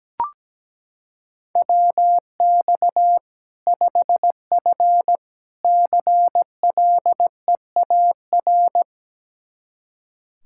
Morse code, heard as a keyed tone, about 17 wpm, 700 Hz. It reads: WX 5F CLEAR
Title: Practice QSO - Exchange 2 - Info - 17wpm